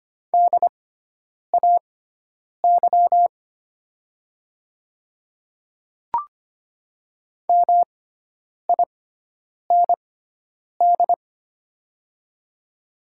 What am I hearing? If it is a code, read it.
Morse code, heard as a keyed tone, about 25 wpm, spaced as for 9 wpm, 700 Hz. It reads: DAY MIND
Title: 6x Character Spacing - Top 500 Words - 25wpm